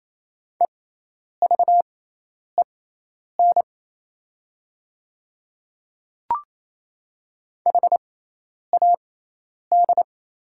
Morse code, heard as a keyed tone, about 28 wpm, spaced as for 10 wpm, 700 Hz. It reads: EVEN HAD
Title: 6x Character Spacing - Top 200 Words - 28wpm